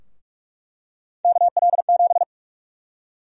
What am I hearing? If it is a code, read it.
Morse code, heard as a keyed tone, about 42 wpm, 700 Hz. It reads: KL7
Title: USA Call Sign Prefixes - Repeated 2x - 45wpm